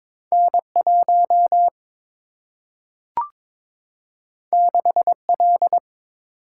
Morse code, heard as a keyed tone, about 22 wpm, 700 Hz. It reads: N 1 6 L
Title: Single-Letter Single-Number 22wpm